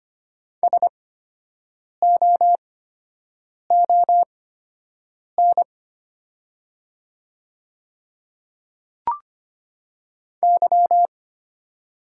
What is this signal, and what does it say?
Morse code, heard as a keyed tone, about 25 wpm, spaced as for 7 wpm, 690 Hz. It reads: SOON Y
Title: 8x Character Spacing - Top 500 Words - 25wpm